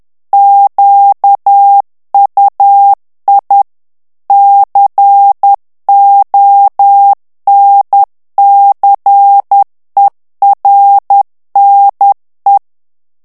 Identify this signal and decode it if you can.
Morse code, heard as a keyed tone, about 10 wpm, 800 Hz. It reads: QUI CONCERNE
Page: Français - texte en clair 1 - vitesse : 10 mots minute